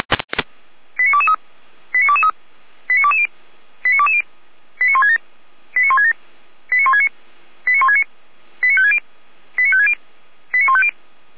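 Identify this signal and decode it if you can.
Five-tone selective call (ZVEI), heard as quick runs of five tones, tones of about 70 ms. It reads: 89223 89223 89220 89220 89167 89167 89169 89169 89560 89560 89150
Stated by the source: Recorded with an SDR, ZVEI five-tone selective call for the Orlamünde fire brigade (Feuerwehr Orlamünde).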